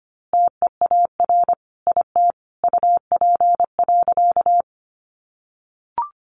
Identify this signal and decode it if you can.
Morse code, heard as a keyed tone, about 25 wpm, 700 Hz. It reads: TEAR IT UP.
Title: A Sound Of Thunder 25wpm